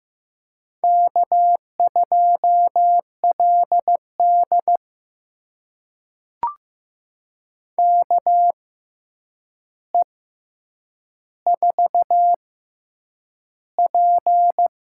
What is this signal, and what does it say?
Morse code, heard as a keyed tone, about 15 wpm, spaced as for 8 wpm, 700 Hz. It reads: K2LD KE4P
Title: USA Call Signs - 4 Characters - 6x Character Spacing - 15wpm